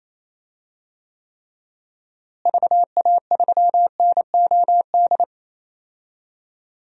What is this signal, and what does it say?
Morse code, heard as a keyed tone, about 27 wpm, 700 Hz. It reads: VA3NOD
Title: DXCC-100 Call Signs - 6 Characters - Repeated 2x - 28wpm